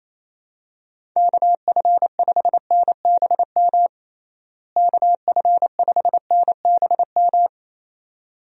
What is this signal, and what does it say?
Morse code, heard as a keyed tone, about 28 wpm, 700 Hz. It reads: KF5NBM KF5NBM
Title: USA Call Signs - 6 Characters - Repeated 2x - 28wpm